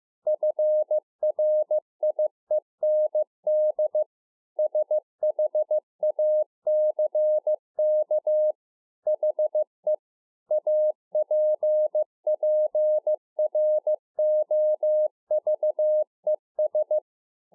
Morse code, heard as a keyed tone, about 15 wpm, 610 Hz. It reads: FRIEND SHACK HE APPROVES